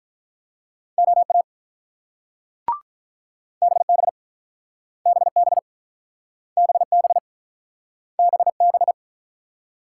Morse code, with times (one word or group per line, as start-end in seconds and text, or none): 1.0-1.4 OA
3.6-4.1 BB
5.1-5.6 BB
6.6-7.2 BB
8.2-8.9 BB